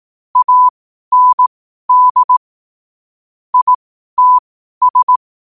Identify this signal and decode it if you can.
Morse code, heard as a keyed tone, about 18 wpm, spaced as for 12 wpm, 1000 Hz. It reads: AND ITS